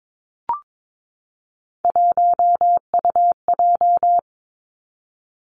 Morse code, heard as a keyed tone, about 22 wpm, 700 Hz. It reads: 1UJ
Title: Three Challenging Letter-Number Combinations 22wpm